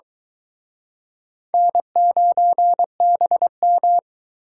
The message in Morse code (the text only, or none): N9BM